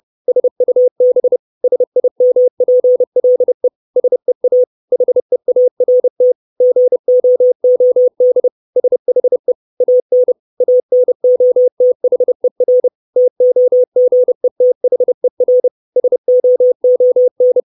Morse code, heard as a keyed tone, about 30 wpm, 500 Hz. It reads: SUB SIMPLE SEA HEART GOOD SHE AN ANOTHER TOGETHER SOON
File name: 632 Words 30WPM